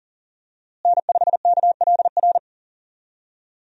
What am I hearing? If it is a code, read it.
Morse code, heard as a keyed tone, about 40 wpm, 700 Hz. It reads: N5KLR
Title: USA Call Signs - 5 Characters - Repeated 3x - 40wpm